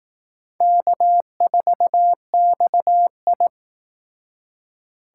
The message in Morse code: K4XI

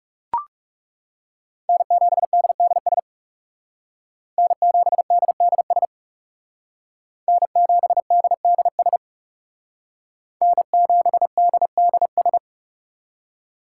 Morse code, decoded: N7DDS N7DDS N7DDS N7DDS

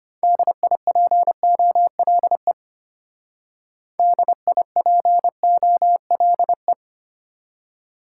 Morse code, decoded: DIPOLE DIPOLE

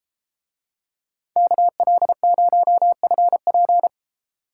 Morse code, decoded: KL0FP

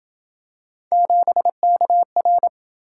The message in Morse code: 7KR

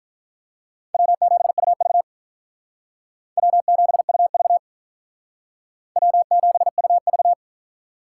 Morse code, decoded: W7UV W7UV W7UV

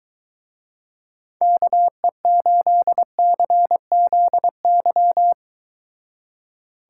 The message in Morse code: KE8CZY

